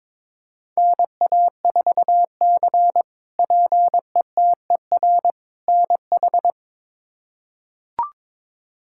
Morse code, heard as a keyed tone, about 21 wpm, 700 Hz. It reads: NA4C PETER NH
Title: K1USN SST QSO Practice - 22wpm